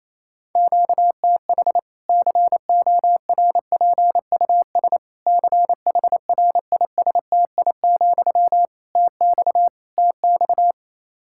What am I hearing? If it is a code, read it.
Morse code, heard as a keyed tone, about 28 wpm, 700 Hz. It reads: QTH CORPUS CHRISTI, TX TX
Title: Practice QSO - Exchange 1 - Beginning - 28wpm